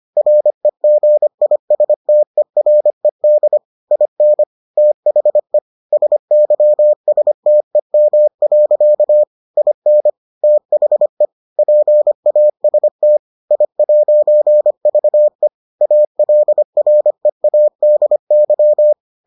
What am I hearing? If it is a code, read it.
Morse code, heard as a keyed tone, about 25 wpm, 600 Hz. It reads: REGISTERED IN THE SYSTEM. IN THE PAST I'VE ALREADY